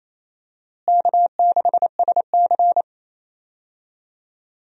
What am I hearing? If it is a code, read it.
Morse code, heard as a keyed tone, about 28 wpm, 700 Hz. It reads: K6SC
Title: USA Call Signs - 4 Characters - Repeated 2x - 28wpm